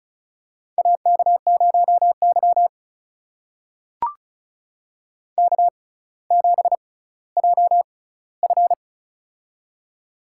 Morse code, heard as a keyed tone, about 36 wpm, spaced as for 18 wpm, 700 Hz. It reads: AK0Y K7JF
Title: USA Call Signs - 4 Characters - 6x Character Spacing - 35wpm